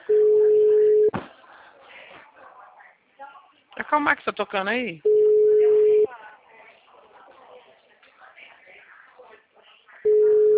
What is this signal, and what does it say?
Ringback tone (Europe, 425 Hz), with speech.